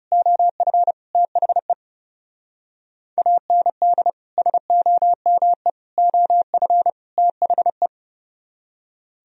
Morse code, heard as a keyed tone, about 37 wpm, 700 Hz. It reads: OF THE AND SOME OF THE
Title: Speed-Racing - Sets of 4 Words 30wpm